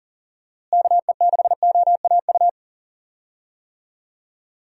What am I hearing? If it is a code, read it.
Morse code, heard as a keyed tone, about 40 wpm, 700 Hz. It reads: KE6OAU
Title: USA Call Signs - 6 Characters - Repeated 3x - 40wpm